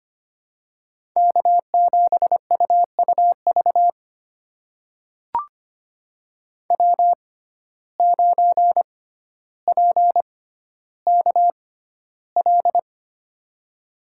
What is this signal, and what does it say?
Morse code, heard as a keyed tone, about 25 wpm, spaced as for 13 wpm, 700 Hz. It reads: K7UUV W9PKL
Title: USA Call Signs - 6x Character Spacing - 25wpm